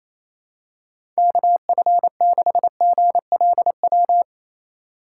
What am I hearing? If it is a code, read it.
Morse code, heard as a keyed tone, about 28 wpm, 700 Hz. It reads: KF6GLW
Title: USA Call Signs - 6 Characters - 2x Character Spacing - 28wpm